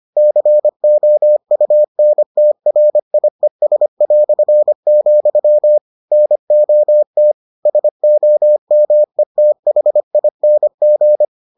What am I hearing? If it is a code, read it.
Morse code, heard as a keyed tone, about 25 wpm, 600 Hz. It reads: COUNTRIES", NOT SOMETHING